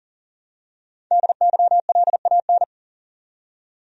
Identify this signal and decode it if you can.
Morse code, heard as a keyed tone, about 40 wpm, 700 Hz. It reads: DYLAN
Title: USA Common Names - Max 5 Letters - 40wpm